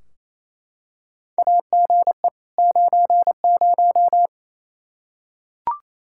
Morse code, heard as a keyed tone, about 28 wpm, 700 Hz. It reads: AGE 90
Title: K1USN SST QSO Practice 28wpm